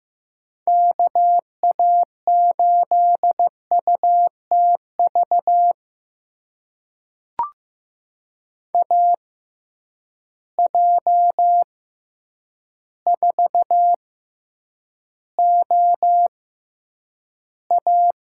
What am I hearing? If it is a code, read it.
Morse code, heard as a keyed tone, about 15 wpm, spaced as for 8 wpm, 700 Hz. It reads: KA8UTV AJ4OA
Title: USA Call Signs - 6x Character Spacing - 15wpm